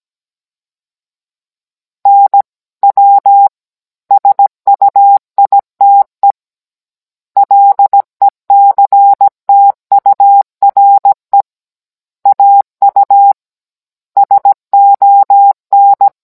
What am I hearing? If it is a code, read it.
Morse code, heard as a keyed tone, about 17 wpm, 800 Hz. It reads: NW SUITE LECTURE AU SON
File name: Cours-NR19-16mm